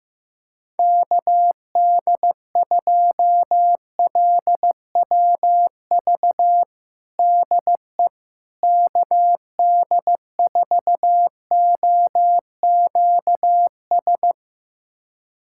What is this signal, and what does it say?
Morse code, heard as a keyed tone, about 15 wpm, 700 Hz. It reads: KD2LWV DE KD4OQS
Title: USA Callsign DE Callsign - 6 Characters - 15wpm